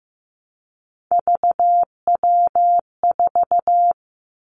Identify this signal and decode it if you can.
Morse code, heard as a keyed tone, about 15 wpm, 700 Hz. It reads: VW4
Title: Three Challenging Letter-Number Combinations 15wpm